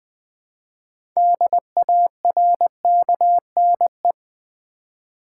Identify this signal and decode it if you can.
Morse code, heard as a keyed tone, about 20 wpm, 700 Hz. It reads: DARKNE
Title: Build A Word - 2 Letters At A Time - 8-Letter Words - 20wpm